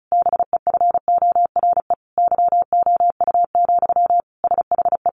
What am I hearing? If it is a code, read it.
Morse code, heard as a keyed tone, about 34 wpm, 700 Hz. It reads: BEFORE YOU, SHE
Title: Sentences from Top 400 Words 35wpm